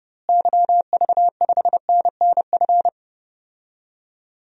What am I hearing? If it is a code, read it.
Morse code, heard as a keyed tone, about 30 wpm, 700 Hz. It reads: YV5NNF
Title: DXCC-100 Call Signs - 6 Characters - Repeated 3x - 30wpm